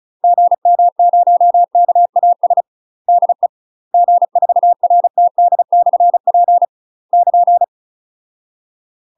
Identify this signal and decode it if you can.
Morse code, heard as a keyed tone, about 35 wpm, 700 Hz. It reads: GM0KAS DE G4RTD/P <KN>
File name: UK MASTER QSOs TRAINING / 35WPM / UK MASTER QSOs 0029